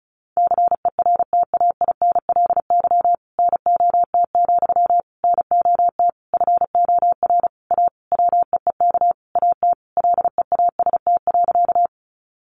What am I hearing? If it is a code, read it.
Morse code, heard as a keyed tone, about 34 wpm, 700 Hz. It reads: CERTAINLY NOT, NOT FOR A WEEK AT LEAST.
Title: Sentences from Top 900 Words 35wpm